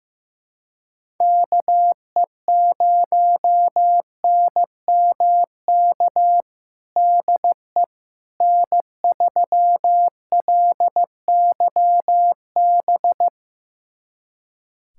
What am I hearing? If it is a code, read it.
Morse code, heard as a keyed tone, about 15 wpm, 700 Hz. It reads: KE0NMK DE N3LYB
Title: USA Callsign DE Callsign 15wpm